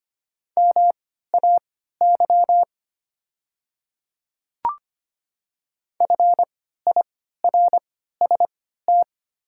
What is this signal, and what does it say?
Morse code, heard as a keyed tone, about 25 wpm, spaced as for 14 wpm, 700 Hz. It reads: MAY FIRST
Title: 3x Character Spacing - Top 100 Words - 25wpm